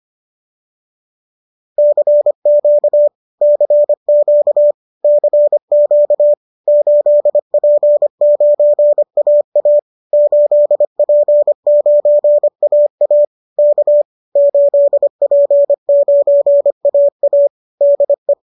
Morse code, heard as a keyed tone, about 25 wpm, 590 Hz. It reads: CQ CQ CQ 8P9AA 8P9AA K 8P9AA DE